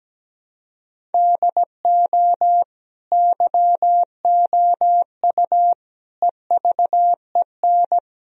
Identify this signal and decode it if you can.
Morse code, heard as a keyed tone, about 17 wpm, 700 Hz. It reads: DO YOU EVEN
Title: Sentences from Top 1000 Words - Review Part 1 - 17wpm